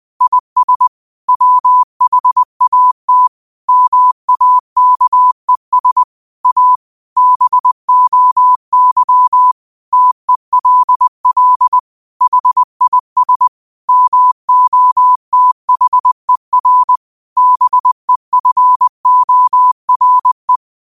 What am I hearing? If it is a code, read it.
Morse code, heard as a keyed tone, about 20 wpm, 1000 Hz. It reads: IS WHAT MAKES A BOY TELL HIS MOTHER BEFORE
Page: Quotes for Tue, 12 Aug 2025 in Morse Code at 20 words per minute.